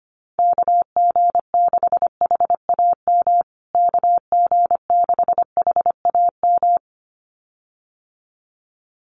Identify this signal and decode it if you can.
Morse code, heard as a keyed tone, about 25 wpm, 700 Hz. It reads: KG6HAM KG6HAM
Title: USA Call Signs Repeated 2x 25wpm